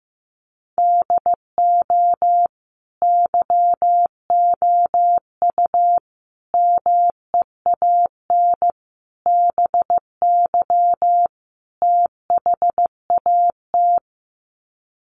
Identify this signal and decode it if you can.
Morse code, heard as a keyed tone, about 15 wpm, 700 Hz. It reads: DO YOU MEAN BY THAT
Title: Sets of 5 Words Encore2 15wpm